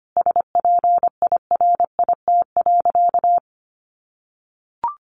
Morse code, heard as a keyed tone, about 25 wpm, 700 Hz. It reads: SPIRIT.
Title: Sentences from Top 1400 Words 25wpm